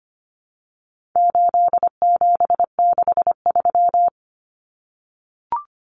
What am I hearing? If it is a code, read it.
Morse code, heard as a keyed tone, about 25 wpm, 700 Hz. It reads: 8763
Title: Common Cut Numbers - 4-Digit 25wpm